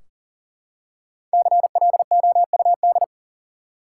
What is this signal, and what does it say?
Morse code, heard as a keyed tone, about 40 wpm, 700 Hz. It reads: CLOUD
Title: Five Letter Words - 40wpm